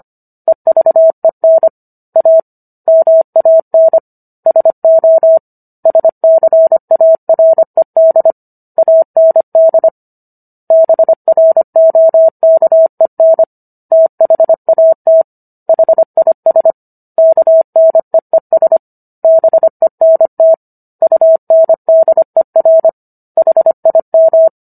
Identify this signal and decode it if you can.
Morse code, heard as a keyed tone, about 25 wpm, 650 Hz. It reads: EVEN A MAN SO SCARED AND BROKEN THAT HIS KNEES BENT UNDER HIM